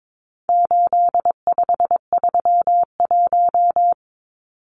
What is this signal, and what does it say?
Morse code, heard as a keyed tone, about 22 wpm, 700 Hz. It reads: 8531
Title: Numbers - 4-Digit 22wpm